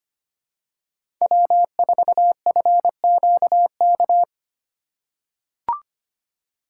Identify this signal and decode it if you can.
Morse code, heard as a keyed tone, about 25 wpm, 700 Hz. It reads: W4FQK
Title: USA Call Signs - 5 Characters - 3x Character Spacing - 25wpm